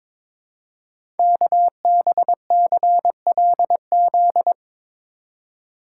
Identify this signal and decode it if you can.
Morse code, heard as a keyed tone, about 22 wpm, 700 Hz. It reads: KBCLZ
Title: Code Groups - 5 Letters - 22wpm